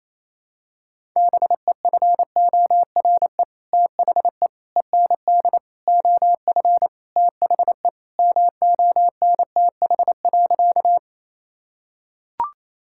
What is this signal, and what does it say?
Morse code, heard as a keyed tone, about 28 wpm, 700 Hz. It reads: BEFORE THE END OF THE MONTH.
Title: Sentences from Top 1000 Words - Review Part 5 - 28wpm